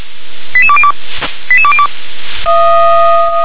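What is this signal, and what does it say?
Five-tone selective call (ZVEI), heard as quick runs of five tones, tones of about 70 ms. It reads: 88202 88202
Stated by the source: ZVEI five-tone selective call received with an SDR.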